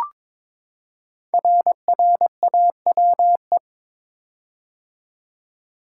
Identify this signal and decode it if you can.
Morse code, heard as a keyed tone, about 22 wpm, 700 Hz. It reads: RRAWE
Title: Code Groups - 5 Letters - 22wpm